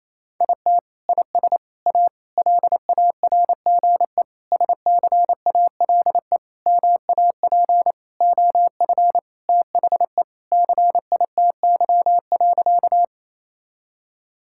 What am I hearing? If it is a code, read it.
Morse code, heard as a keyed tone, about 28 wpm, 700 Hz. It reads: IT IS A LARGE SCALE MAP OF THE CITY.
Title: Sentences from Top 1000 Words - Review Part 5 - 28wpm